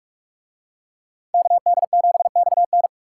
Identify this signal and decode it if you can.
Morse code, heard as a keyed tone, about 42 wpm, 700 Hz. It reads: KD7XN